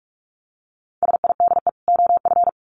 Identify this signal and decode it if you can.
Morse code, heard as a keyed tone, about 44 wpm, 700 Hz. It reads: SIDE OF
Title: Sets of 2 Words 45wpm